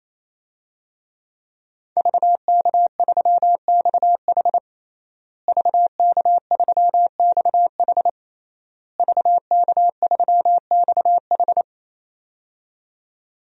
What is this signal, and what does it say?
Morse code, heard as a keyed tone, about 27 wpm, 700 Hz. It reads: VK3XH VK3XH VK3XH
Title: DXCC-100 Call Signs - 5 Characters - Repeated 3x - 28wpm